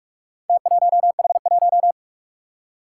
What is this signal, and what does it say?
Morse code, heard as a keyed tone, about 46 wpm, 700 Hz. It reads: T1H1
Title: Four Challenging Letter-Number Combinations 45wpm